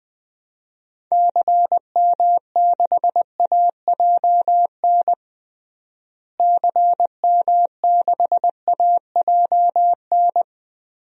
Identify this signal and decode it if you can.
Morse code, heard as a keyed tone, about 21 wpm, 700 Hz. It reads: CM6AJN CM6AJN
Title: DXCC-100 Call Signs - 6 Characters - Repeated 3x - 20wpm